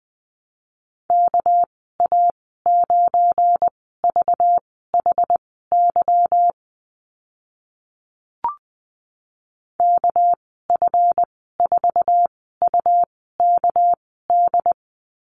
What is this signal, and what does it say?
Morse code, heard as a keyed tone, about 20 wpm, spaced as for 14 wpm, 700 Hz. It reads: KA9VHY KF4UKD
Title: 213 Lesson 34g - Number 8 - Call Signs 20wpm